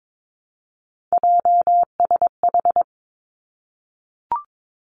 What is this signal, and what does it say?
Morse code, heard as a keyed tone, about 22 wpm, 700 Hz. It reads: JSH
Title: Three Challenging Letter-Number Combinations 22wpm